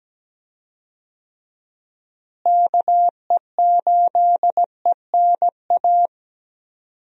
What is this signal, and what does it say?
Morse code, heard as a keyed tone, about 17 wpm, 700 Hz. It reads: KE8ENA
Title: USA Call Signs - 6 Characters - Repeated 3x - 17wpm